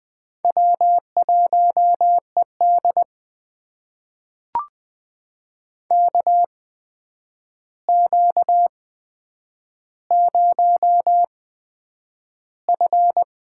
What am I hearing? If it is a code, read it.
Morse code, heard as a keyed tone, about 20 wpm, spaced as for 9 wpm, 700 Hz. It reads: W1ED KQ0F
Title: USA Call Signs - 4 Characters - 8x Character Spacing - 20wpm